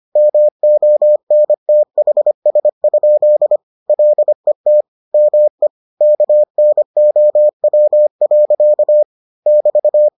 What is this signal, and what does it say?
Morse code, heard as a keyed tone, about 25 wpm, 600 Hz. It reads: MONTHS? LET ME KNOW. <BT>